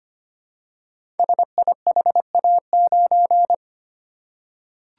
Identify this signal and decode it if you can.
Morse code, heard as a keyed tone, about 25 wpm, 700 Hz. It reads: SIHA9
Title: Code Groups - 5 Letter:Numbers - 25wpm